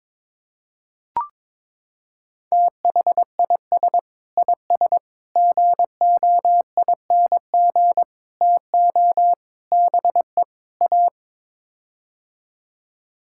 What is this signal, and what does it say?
Morse code, heard as a keyed tone, about 22 wpm, 700 Hz. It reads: THIS IS GOING TO BE A
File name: practice / 22wpm / Sets of 6 Words - 22wpm